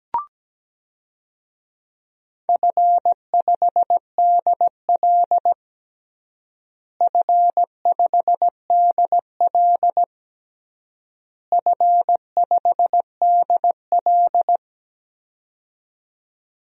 Morse code, heard as a keyed tone, about 17 wpm, 700 Hz. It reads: F5DL F5DL F5DL
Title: DXCC-100 Call Signs - 4 Characters - Repeated 3x - 17wpm